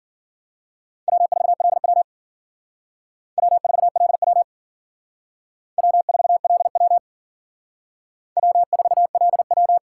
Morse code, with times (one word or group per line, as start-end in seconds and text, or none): 1.1-2.0 W4LW
3.4-4.4 W4LW
5.8-7.0 W4LW
8.4-9.8 W4LW